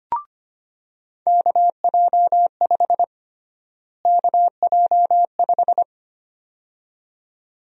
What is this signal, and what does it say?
Morse code, heard as a keyed tone, about 25 wpm, 700 Hz. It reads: KJ5 KJ5
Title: USA Call Sign Prefixes - Repeated 2x - 25wpm